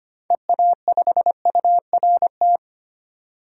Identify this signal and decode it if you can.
Morse code, heard as a keyed tone, about 25 wpm, 700 Hz. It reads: EA5URT
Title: DXCC-100 Call Signs - 6 Characters - 25wpm